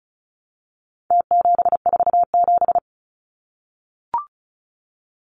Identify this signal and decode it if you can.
Morse code, heard as a keyed tone, about 34 wpm, 700 Hz. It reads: T747
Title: Numbers - Contest Style - 4-Digit 35wpm